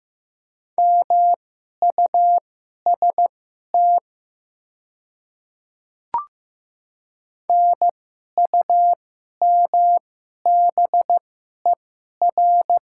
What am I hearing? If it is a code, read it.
Morse code, heard as a keyed tone, about 15 wpm, spaced as for 11 wpm, 700 Hz. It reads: MUST NUMBER
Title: 2x Character Spacing - Top 200 Words - 15wpm